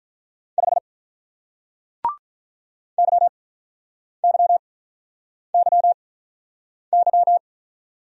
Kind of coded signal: Morse code

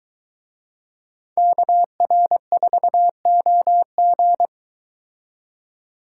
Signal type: Morse code

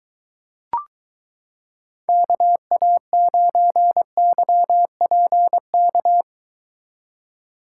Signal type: Morse code

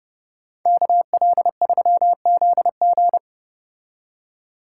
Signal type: Morse code

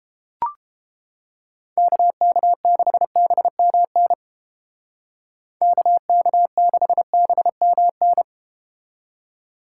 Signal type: Morse code

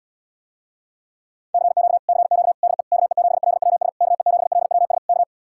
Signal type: Morse code

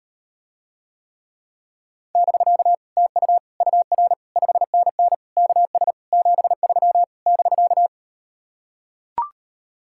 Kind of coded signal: Morse code

